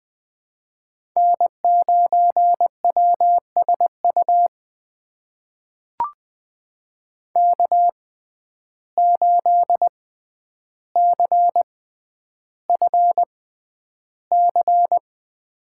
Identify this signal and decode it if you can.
Morse code, heard as a keyed tone, about 20 wpm, spaced as for 10 wpm, 700 Hz. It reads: N9WSU K8CFC
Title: USA Call Signs - 5 Characters - 6x Character Spacing - 20wpm